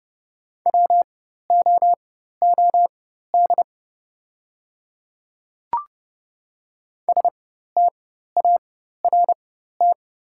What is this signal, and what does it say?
Morse code, heard as a keyed tone, about 30 wpm, spaced as for 14 wpm, 700 Hz. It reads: WOOD START
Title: 4x Character Spacing - Top 500 Words - 30wpm